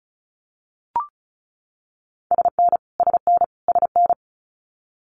Morse code, heard as a keyed tone, about 36 wpm, 700 Hz. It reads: SN SN SN
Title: Multi-letter Phonemes Repeated 3x 35wpm